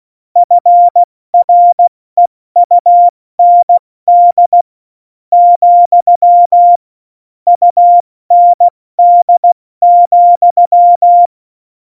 Morse code, heard as a keyed tone, about 16 wpm, 700 Hz. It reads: FREUND , UND,
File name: Morsekurs / Witze / Witz 20. August 2019, 70 F80